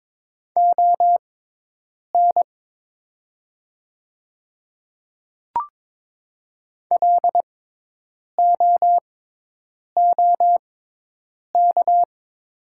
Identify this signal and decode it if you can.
Morse code, heard as a keyed tone, about 22 wpm, spaced as for 8 wpm, 700 Hz. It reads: ON LOOK
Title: 6x Character Spacing - Top 100 Words - 22wpm